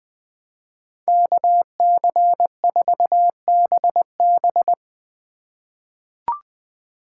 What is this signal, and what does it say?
Morse code, heard as a keyed tone, about 21 wpm, 700 Hz. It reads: KC4BB
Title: 2023 POTA Activator Call Signs - 20wpm